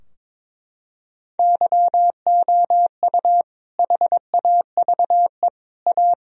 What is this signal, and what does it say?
Morse code, heard as a keyed tone, about 22 wpm, 700 Hz. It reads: YOU HAVE A
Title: Sentences from Top 1000 Words - Review Part 3 - 22wpm